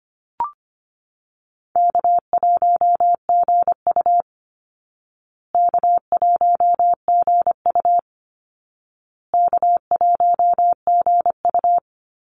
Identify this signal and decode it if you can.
Morse code, heard as a keyed tone, about 24 wpm, 700 Hz. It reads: K1GU K1GU K1GU
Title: Call Signs Repeated 2x - 25wpm